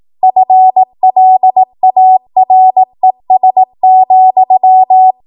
Morse code, heard as a keyed tone, about 18 wpm, 750 Hz. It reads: FLARES,
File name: Morse / 18WPM